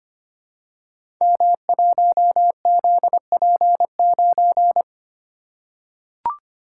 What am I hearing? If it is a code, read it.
Morse code, heard as a keyed tone, about 25 wpm, 700 Hz. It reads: M1ZP9
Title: Code Groups - 5 Letter:Numbers - 25wpm